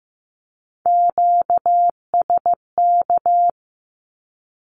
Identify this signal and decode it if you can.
Morse code, heard as a keyed tone, about 15 wpm, 700 Hz. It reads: QSK
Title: Q-Codes with Spoken Meaning - 15wpm